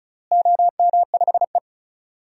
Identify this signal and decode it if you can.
Morse code, heard as a keyed tone, about 36 wpm, 700 Hz. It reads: OM5E
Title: DXCC-100 Call Signs - 4 Characters - Repeated 3x - 35wpm